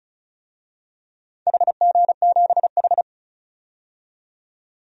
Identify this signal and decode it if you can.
Morse code, heard as a keyed tone, about 36 wpm, 690 Hz. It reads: HG7H
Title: DXCC-100 Call Signs - 4 Characters - 35wpm